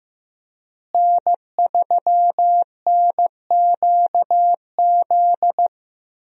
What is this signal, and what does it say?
Morse code, heard as a keyed tone, about 15 wpm, 700 Hz. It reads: N3NQZ
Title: USA Call Signs - 5 Characters - Repeated 3x - 15wpm